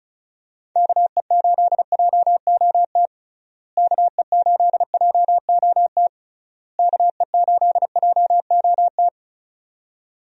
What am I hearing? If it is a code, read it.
Morse code, heard as a keyed tone, about 36 wpm, 700 Hz. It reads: KE8JOT KE8JOT KE8JOT
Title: USA Call Signs - 6 Characters - Repeated 3x - 35wpm